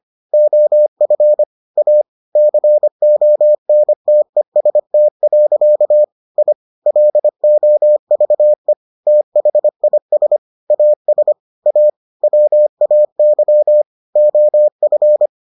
Morse code, heard as a keyed tone, about 25 wpm, 600 Hz. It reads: OF A CONTEST. I LOVE THIS AS A WAY OF